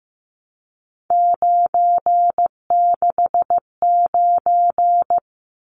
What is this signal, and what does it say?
Morse code, heard as a keyed tone, about 15 wpm, 700 Hz. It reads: 969
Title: Numbers - 3-Digit 15wpm